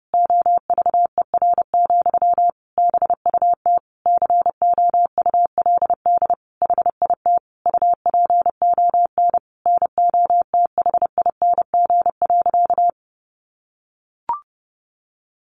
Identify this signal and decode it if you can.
Morse code, heard as a keyed tone, about 30 wpm, 700 Hz. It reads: OVER, BUT COULD HIT UPON NOTHING.
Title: Sentences from Top 1200 Words 30wpm